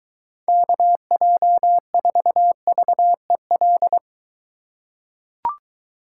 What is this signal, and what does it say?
Morse code, heard as a keyed tone, about 23 wpm, 700 Hz. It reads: KJ4VEL